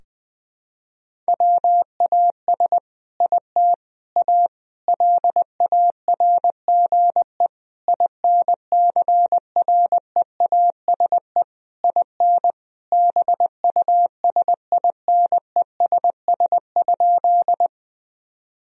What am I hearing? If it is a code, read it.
Morse code, heard as a keyed tone, about 20 wpm, 700 Hz. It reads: WAS IT A LARGE INCREASE IN BUSINESS?
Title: Sentences from Top 1000 Words - Review Part 4 - 20wpm